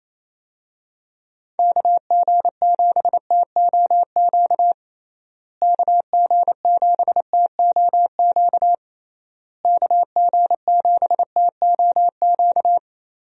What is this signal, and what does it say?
Morse code, heard as a keyed tone, about 27 wpm, 700 Hz. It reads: KG7TOQ KG7TOQ KG7TOQ
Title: USA Call Signs - 6 Characters - Repeated 3x - 28wpm